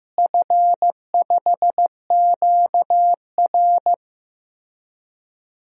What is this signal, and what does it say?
Morse code, heard as a keyed tone, about 15 wpm, 700 Hz. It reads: F5QR
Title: DXCC-100 Call Signs - 4 Characters - Repeated 2x - 15wpm